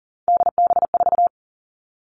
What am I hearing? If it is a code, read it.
Morse code, heard as a keyed tone, about 40 wpm, 700 Hz. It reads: DB4
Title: Three Challenging Letter-Number Combinations 40wpm